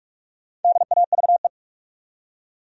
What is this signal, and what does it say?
Morse code, heard as a keyed tone, about 42 wpm, 700 Hz. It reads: DAVE
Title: USA Common Names - 45wpm